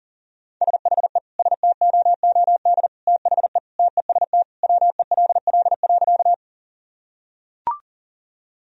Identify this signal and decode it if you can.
Morse code, heard as a keyed tone, about 40 wpm, 700 Hz. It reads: SHE STOOD THE TEST WELL.
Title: Sentences from Top 1000 Words - Review Part 3 - 40wpm